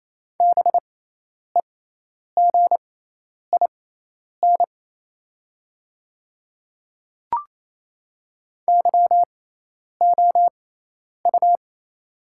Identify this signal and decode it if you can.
Morse code, heard as a keyed tone, about 27 wpm, spaced as for 10 wpm, 700 Hz. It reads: BEGIN YOU
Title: 6x Character Spacing - Top 300 Words - 28wpm